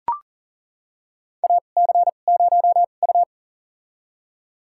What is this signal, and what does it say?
Morse code, heard as a keyed tone, about 38 wpm, spaced as for 28 wpm, 690 Hz. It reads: AC0U
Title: USA Call Signs - 4 Characters - 2x Character Spacing - 40wpm